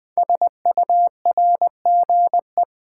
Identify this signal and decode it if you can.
Morse code, heard as a keyed tone, about 20 wpm, 700 Hz. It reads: SURGE
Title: 5 Letter Words - First 3 Letters Given - 20wpm